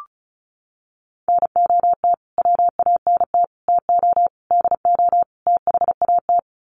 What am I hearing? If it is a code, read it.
Morse code, heard as a keyed tone, about 34 wpm, 700 Hz. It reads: NOT WANT TO DO THAT
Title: Sets of 5 Words Encore 35wpm